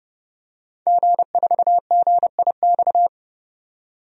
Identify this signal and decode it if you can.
Morse code, heard as a keyed tone, about 30 wpm, 700 Hz. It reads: G4GIX
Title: DXCC-100 Call Signs - 5 Characters - 30wpm